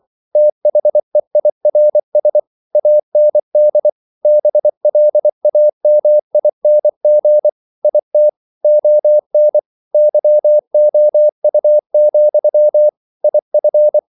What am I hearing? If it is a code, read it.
Morse code, heard as a keyed tone, about 24 wpm, 600 Hz. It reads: THEIRS AND BLAMING IT ON YOU, IF
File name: CW / Practice / If24